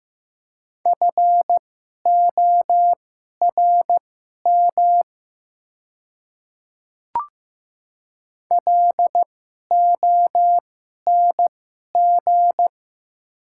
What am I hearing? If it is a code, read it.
Morse code, heard as a keyed tone, about 15 wpm, spaced as for 11 wpm, 700 Hz. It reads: FORM LONG
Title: 2x Character Spacing - Top 200 Words - 15wpm